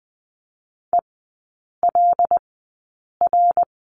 Morse code, heard as a keyed tone, about 20 wpm, spaced as for 8 wpm, 700 Hz. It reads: ELR
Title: 074 Lesson 13e - Letter L - Three Characters 20wpm